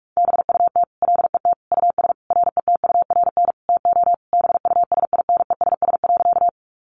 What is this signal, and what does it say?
Morse code, heard as a keyed tone, about 44 wpm, 700 Hz. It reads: BUT LET US RETURN TO BUSINESS.
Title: Sentences from Top 800 Words 45wpm